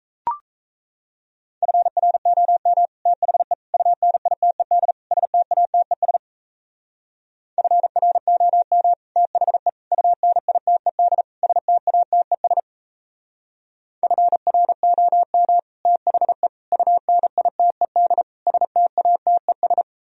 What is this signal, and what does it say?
Morse code, heard as a keyed tone, about 33 wpm, 700 Hz. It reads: FROM THE UNITED STATES FROM THE UNITED STATES FROM THE UNITED STATES
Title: Speed-Racing - Sets of 4 Words 28wpm